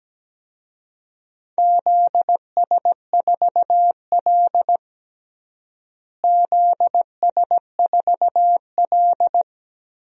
Morse code, heard as a keyed tone, about 17 wpm, 700 Hz. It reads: ZS4L ZS4L
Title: DXCC-100 Call Signs - 4 Characters - Repeated 2x - 17wpm